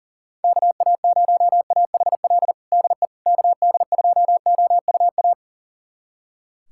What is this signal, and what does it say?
Morse code, heard as a keyed tone, about 37 wpm, 700 Hz. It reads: KA0AHL DE KD2OUA
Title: USA Callsign DE Callsign 40wpm